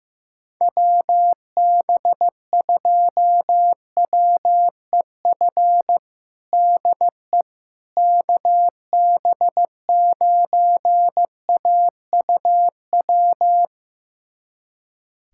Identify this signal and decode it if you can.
Morse code, heard as a keyed tone, about 15 wpm, 700 Hz. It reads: WB2WEF DE KB9AUW
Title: USA Callsign DE Callsign 15wpm